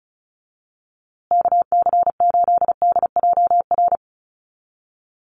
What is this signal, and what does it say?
Morse code, heard as a keyed tone, about 34 wpm, 700 Hz. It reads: KC8DJR
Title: USA Call Signs Repeated 2x 35wpm